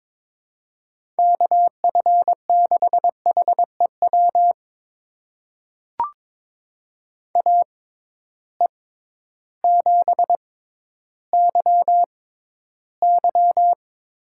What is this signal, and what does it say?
Morse code, heard as a keyed tone, about 21 wpm, spaced as for 12 wpm, 700 Hz. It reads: KF6HEW AE7YY
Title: USA Call Signs - 6x Character Spacing - 22wpm